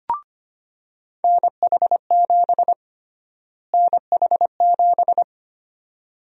Morse code, heard as a keyed tone, about 25 wpm, 700 Hz. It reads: NH7 NH7
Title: USA Call Sign Prefixes - Repeated 2x - 25wpm